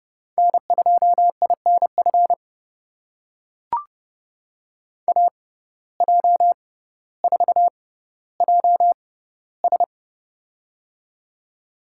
Morse code, heard as a keyed tone, about 30 wpm, spaced as for 15 wpm, 700 Hz. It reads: N2INF AJ4JS
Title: USA Call Signs - 5 Characters - 6x Character Spacing - 30wpm